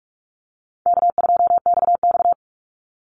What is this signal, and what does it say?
Morse code, heard as a keyed tone, about 46 wpm, 700 Hz. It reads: K2XX
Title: Easy Call Signs - 45wpm